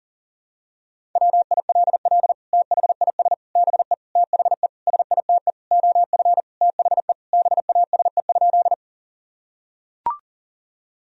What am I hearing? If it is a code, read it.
Morse code, heard as a keyed tone, about 37 wpm, 700 Hz. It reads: WILL THIS BE THE SITE OF THE BASE?
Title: Sentences from Top 1000 Words - Review Part 1 - 40wpm